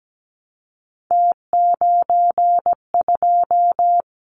Morse code, heard as a keyed tone, about 17 wpm, 700 Hz. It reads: T92
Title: Callsign Prefixes for Top 50 DXCC - 17wpm